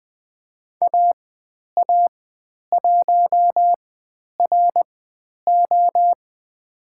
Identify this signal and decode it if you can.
Morse code, heard as a keyed tone, about 20 wpm, spaced as for 10 wpm, 700 Hz. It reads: AA1RO
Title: 109 Lesson 18g - Letter m - Call Signs 20wpm